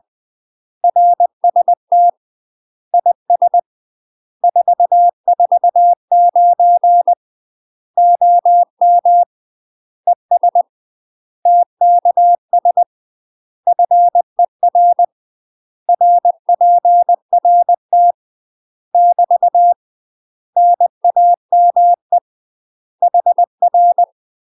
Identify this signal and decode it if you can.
Morse code, heard as a keyed tone, about 20 wpm, 700 Hz. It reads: RST IS 449 OM ES TKS FER RPRT <BT> NAME HR
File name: UK MASTER QSOs TRAINING/20WPM